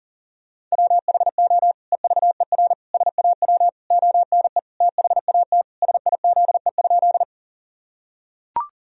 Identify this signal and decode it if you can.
Morse code, heard as a keyed tone, about 37 wpm, 700 Hz. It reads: WHO EVER SAW ONE THAT SIZE?
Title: Sentences from Top 1000 Words - Review Part 2 - 40wpm